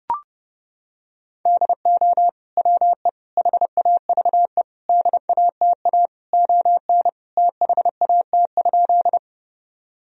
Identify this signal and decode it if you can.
Morse code, heard as a keyed tone, about 32 wpm, 700 Hz. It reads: DO WE HAVE DATA ON THAT?
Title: Sentences from Top 1000 Words - Review Part 3 - 30wpm